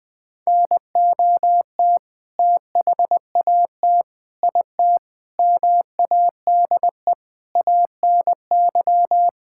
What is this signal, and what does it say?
Morse code, heard as a keyed tone, about 20 wpm, 700 Hz. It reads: NOT THAT IT MADE ANY
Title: Sentences from Top 1000 Words - Review Part 5 - 20wpm